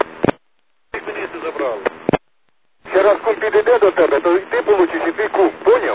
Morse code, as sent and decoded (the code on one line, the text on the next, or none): none